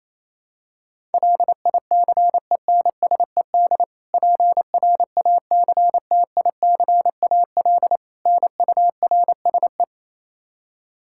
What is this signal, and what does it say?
Morse code, heard as a keyed tone, about 28 wpm, 700 Hz. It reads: LICENSED PRACTICAL NURSE
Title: Practice QSO - Exchange 3 - More Info - 28wpm